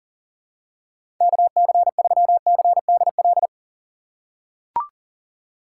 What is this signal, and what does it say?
Morse code, heard as a keyed tone, about 39 wpm, 700 Hz. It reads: KC3CDL
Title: USA Call Signs - 6 Characters - 6x Character Spacing - 40wpm